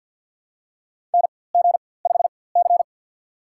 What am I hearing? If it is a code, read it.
Morse code, heard as a keyed tone, about 49 wpm, spaced as for 23 wpm, 700 Hz. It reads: NG5C
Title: USA Call Signs - 4x Character Spacing - 50wpm